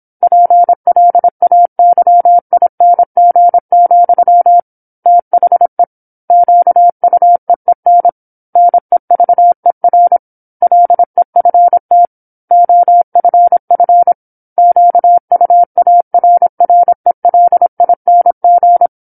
Morse code, heard as a keyed tone, about 27 wpm, 700 Hz. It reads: PLAYING, THE QUEEN NEVER LEFT OFF QUARRELING
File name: 08-Chapter-08 26 WPM